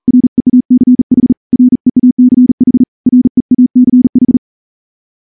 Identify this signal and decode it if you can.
Morse code, heard as a keyed tone, about 39 wpm, 250 Hz. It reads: REACH REACH REACH
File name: morse-training-40WPM